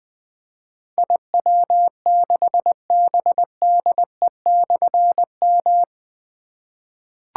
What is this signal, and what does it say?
Morse code, heard as a keyed tone, about 20 wpm, 700 Hz. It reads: IW6BDE/M
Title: Difficult Call Signs 20wpm